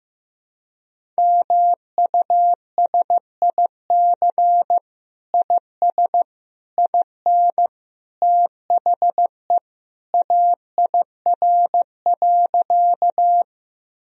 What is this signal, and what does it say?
Morse code, heard as a keyed tone, about 15 wpm, 700 Hz. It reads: MUSIC IS IN THE AIR.
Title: Sentences from Top 1000 Words - Review Part 1 - 15wpm